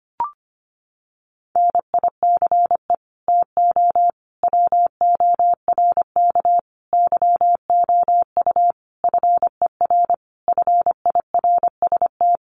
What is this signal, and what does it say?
Morse code, heard as a keyed tone, about 25 wpm, 700 Hz. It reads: NICE TO WORK YOU FER FIRST
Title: Common QSO Phrases 25wpm